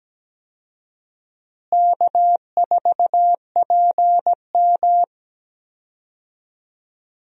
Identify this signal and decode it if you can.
Morse code, heard as a keyed tone, about 17 wpm, 700 Hz. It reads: K4PM
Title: USA Call Signs - 4 Characters - Repeated 3x - 17wpm